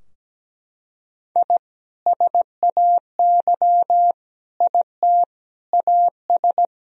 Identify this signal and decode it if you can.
Morse code, heard as a keyed tone, about 17 wpm, 700 Hz. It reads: I SAY IT AS
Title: Sentences from Top 1000 Words - Review Part 2 - 17wpm